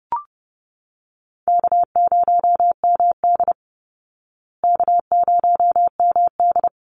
Morse code, heard as a keyed tone, about 30 wpm, 700 Hz. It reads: K0MD K0MD
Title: Easy Call Signs Repeated 2x - 30wpm